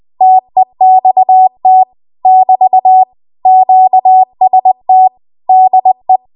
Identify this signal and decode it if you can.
Morse code, heard as a keyed tone, about 20 wpm, 750 Hz. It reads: TEXT <BT> QST DE